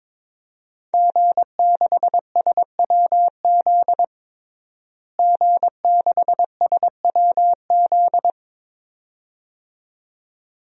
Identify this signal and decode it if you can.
Morse code, heard as a keyed tone, about 22 wpm, 700 Hz. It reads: G6SWZ G6SWZ
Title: DXCC-100 Call Signs - 5 Characters - Repeated 2x - 22wpm